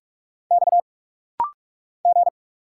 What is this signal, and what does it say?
Morse code, heard as a keyed tone, about 44 wpm, 700 Hz. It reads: X G
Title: Single Letters - Rapid-Fire 45wpm